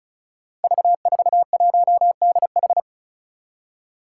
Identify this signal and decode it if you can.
Morse code, heard as a keyed tone, about 36 wpm, 700 Hz. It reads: V41DH
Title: Five Challenging Letter-Number Combinations 35wpm